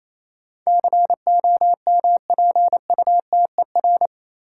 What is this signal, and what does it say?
Morse code, heard as a keyed tone, about 28 wpm, 700 Hz. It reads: COMPUTER
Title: Sentences from Top 1000 Words - Review Part 3 - New Word Drill - 28wpm